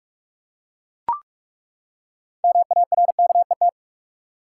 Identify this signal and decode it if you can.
Morse code, heard as a keyed tone, about 42 wpm, 700 Hz. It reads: MARKET
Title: Six Letter Words - 45wpm